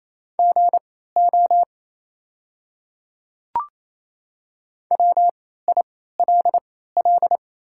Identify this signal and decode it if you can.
Morse code, heard as a keyed tone, about 28 wpm, spaced as for 16 wpm, 700 Hz. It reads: GO WILL
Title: 3x Character Spacing - Top 200 Words - 28wpm